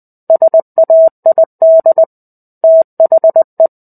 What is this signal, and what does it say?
Morse code, heard as a keyed tone, about 20 wpm, 650 Hz. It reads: SAID THE
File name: 05-Chapter-05 20 WPM